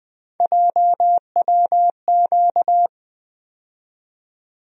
Morse code, heard as a keyed tone, about 20 wpm, 700 Hz. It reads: JWQ
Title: USA Call Sign Postfixes - 3 Letters 20wpm